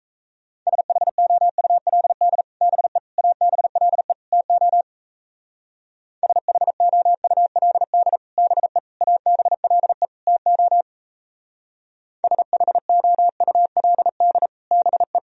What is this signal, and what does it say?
Morse code, heard as a keyed tone, about 32 wpm, 700 Hz. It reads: SHOULD BE ABLE TO SHOULD BE ABLE TO SHOULD BE
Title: Speed-Racing - Sets of 4 Words 28wpm